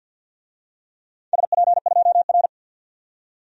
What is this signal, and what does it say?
Morse code, heard as a keyed tone, about 53 wpm, 700 Hz. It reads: SP2R
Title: DXCC-100 Call Signs - 4 Characters - 50wpm